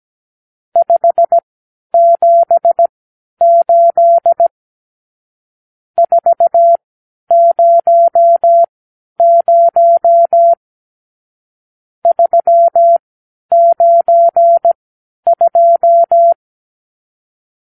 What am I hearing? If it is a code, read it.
Morse code, heard as a keyed tone, about 17 wpm, spaced as for 10 wpm, 680 Hz. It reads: 578 400 392
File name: arrl / Review (1-0)